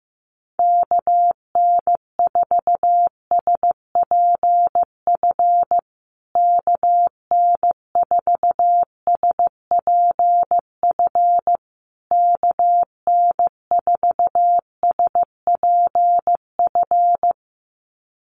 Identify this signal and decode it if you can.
Morse code, heard as a keyed tone, about 15 wpm, 700 Hz. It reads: KN4SPF KN4SPF KN4SPF
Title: USA Call Signs Repeated 3x 15wpm